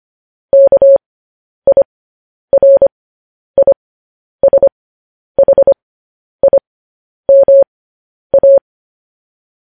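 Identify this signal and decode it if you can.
Morse code, heard as a keyed tone, about 25 wpm, spaced as for 10 wpm, 550 Hz. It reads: KIRISHIMA
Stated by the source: Morse code transmission of news headlines.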